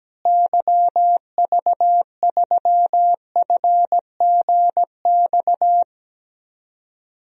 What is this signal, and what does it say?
Morse code, heard as a keyed tone, about 17 wpm, 700 Hz. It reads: YV3FGX
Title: DXCC-100 Call Signs - 6 Characters - Repeated 2x - 17wpm